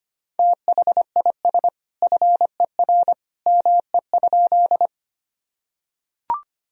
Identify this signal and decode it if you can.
Morse code, heard as a keyed tone, about 25 wpm, 700 Hz. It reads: THIS FER ME?
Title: Sentences from Top 200 Words and Abbreviations 25wpm